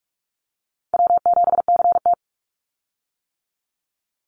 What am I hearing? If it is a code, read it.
Morse code, heard as a keyed tone, about 46 wpm, 700 Hz. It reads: W7CT
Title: Call Signs - 45wpm